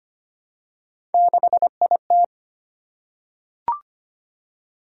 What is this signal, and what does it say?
Morse code, heard as a keyed tone, about 25 wpm, 700 Hz. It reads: 6IT
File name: Three Letter-Number Combinations - 25wpm